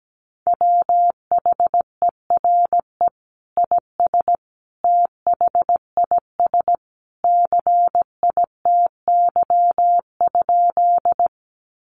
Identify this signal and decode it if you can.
Morse code, heard as a keyed tone, about 17 wpm, 700 Hz. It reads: WHERE IS THIS CITY?
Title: Sentences from Top 300 Words 17wpm